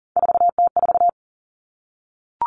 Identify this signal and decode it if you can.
Morse code, heard as a keyed tone, about 41 wpm, 700 Hz. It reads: 4T4
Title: Three Challenging Letter-Number Combinations 40wpm